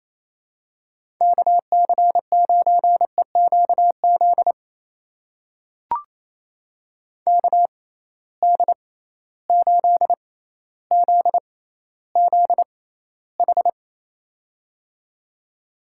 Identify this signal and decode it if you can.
Morse code, heard as a keyed tone, about 28 wpm, spaced as for 14 wpm, 700 Hz. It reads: KC9EQZ KD8ZZH
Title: USA Call Signs - 6 Characters - 6x Character Spacing - 28wpm